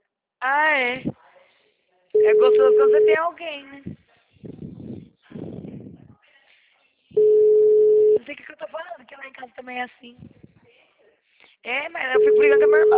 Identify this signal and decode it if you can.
Ringback tone (Europe, 425 Hz), with speech.